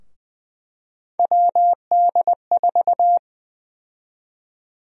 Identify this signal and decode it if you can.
Morse code, heard as a keyed tone, about 20 wpm, 700 Hz. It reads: WD4
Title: USA Call Sign Prefixes - 20wpm